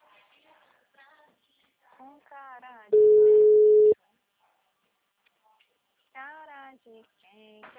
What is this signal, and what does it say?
Ringback tone (Europe, 425 Hz), with speech.